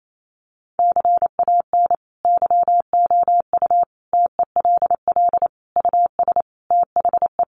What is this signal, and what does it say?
Morse code, heard as a keyed tone, about 28 wpm, 700 Hz. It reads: CAN YOU TELL US THE
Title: Sentences from Top 1300 Words 28wpm